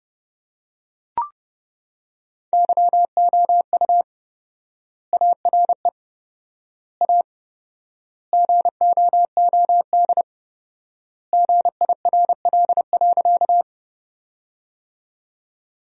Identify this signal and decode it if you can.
Morse code, heard as a keyed tone, about 32 wpm, 700 Hz. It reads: YOU ARE A GOOD GIRL.
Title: 4x Word-Spacing - Sentences from Top 400 Words - 30wpm